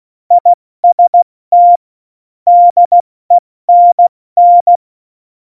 Morse code, heard as a keyed tone, about 16 wpm, 700 Hz. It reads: IST DENN
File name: Morsekurs / Witze / Witz 20 March 2018, 70 F80